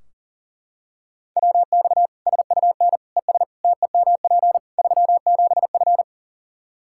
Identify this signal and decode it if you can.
Morse code, heard as a keyed tone, about 40 wpm, 700 Hz. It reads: WX SUN ES TEMP 37F
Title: K1USN SST QSO Practice 40wpm